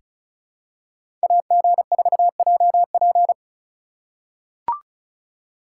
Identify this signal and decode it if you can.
Morse code, heard as a keyed tone, about 34 wpm, 700 Hz. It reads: AG4JP
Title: USA Call Signs - 3x Character Spacing - 35wpm